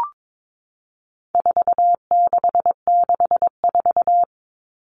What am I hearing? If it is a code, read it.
Morse code, heard as a keyed tone, about 22 wpm, 700 Hz. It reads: 4664
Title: Numbers - 4-Digit 22wpm